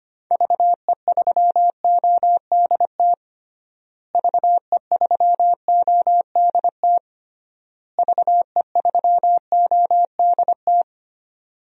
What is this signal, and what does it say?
Morse code, heard as a keyed tone, about 25 wpm, 700 Hz. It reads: VE3ODT VE3ODT VE3ODT
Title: DXCC-100 Call Signs - 6 Characters - Repeated 3x - 25wpm